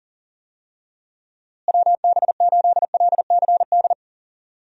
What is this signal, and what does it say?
Morse code, heard as a keyed tone, about 40 wpm, 700 Hz. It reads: WB8LCD
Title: USA Call Signs - 6 Characters - Repeated 2x - 40wpm